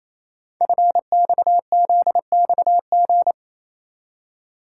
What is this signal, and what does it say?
Morse code, heard as a keyed tone, about 28 wpm, 700 Hz. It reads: FXZXG
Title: Code Groups - 5 Letters - 28wpm